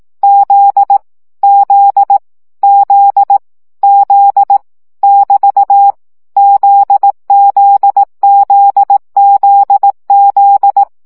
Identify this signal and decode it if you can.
Morse code, heard as a keyed tone, about 18 wpm, 800 Hz. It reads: Z Z Z Z <BT> ZZZZZ
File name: CW gyakorlas z 15 baud